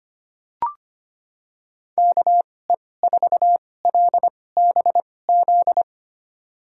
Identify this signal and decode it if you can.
Morse code, heard as a keyed tone, about 25 wpm, spaced as for 18 wpm, 700 Hz. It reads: KE4LBZ
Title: USA Call Signs - 6 Characters - 2x Character Spacing - 25wpm